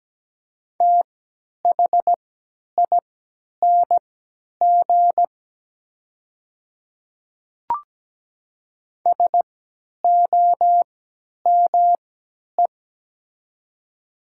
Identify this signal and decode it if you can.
Morse code, heard as a keyed tone, about 17 wpm, spaced as for 10 wpm, 700 Hz. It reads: THING SOME
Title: 3x Character Spacing - Top 200 Words - 17wpm